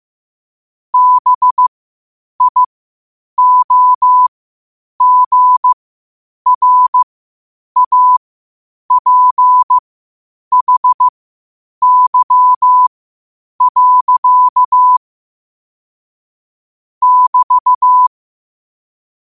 Morse code, heard as a keyed tone, about 15 wpm, spaced as for 8 wpm, 1000 Hz. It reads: BIOGRAPHY. <BT>
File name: qotdfiles / 8wpm